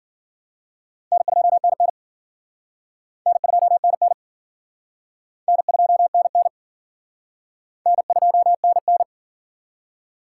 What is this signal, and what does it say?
Morse code, heard as a keyed tone, about 51 wpm, 700 Hz. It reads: N2NN N2NN N2NN N2NN